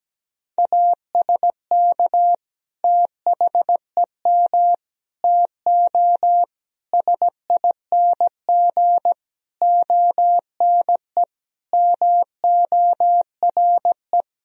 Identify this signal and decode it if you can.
Morse code, heard as a keyed tone, about 17 wpm, 700 Hz. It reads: ASK THEM TO SING ONE MORE
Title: Sentences from Top 1000 Words - Review Part 2 - 17wpm